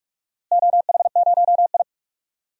Morse code, heard as a keyed tone, about 41 wpm, 700 Hz. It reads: OH0I
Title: DXCC-100 Call Signs - 4 Characters - Repeated 3x - 45wpm